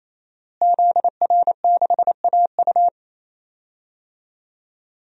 Morse code, heard as a keyed tone, about 27 wpm, 700 Hz. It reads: ZR6AU
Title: DXCC-100 Call Signs - 5 Characters - Repeated 3x - 28wpm